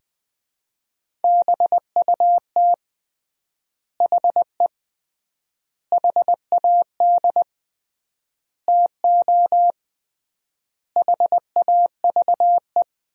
Morse code, heard as a keyed tone, about 20 wpm, 700 Hz. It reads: BUT HE HAD TO HAVE
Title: 3x Word-Spacing - Sentences from Top 100 Words - 20wpm